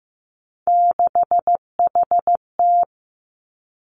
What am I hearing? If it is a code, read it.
Morse code, heard as a keyed tone, about 15 wpm, 700 Hz. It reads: 6HT
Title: Three Challenging Letter-Number Combinations 15wpm